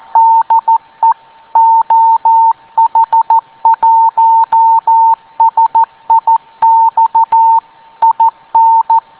Morse code, heard as a keyed tone, about 14 wpm, 910 Hz. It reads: DE OH1SIX IN